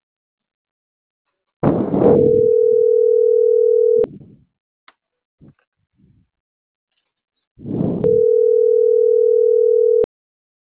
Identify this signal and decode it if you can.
Ringback tone (North America), 440 + 480 Hz.